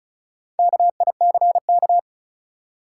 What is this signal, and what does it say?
Morse code, heard as a keyed tone, about 34 wpm, 700 Hz. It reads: KICK
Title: Four Letter Words - 35wpm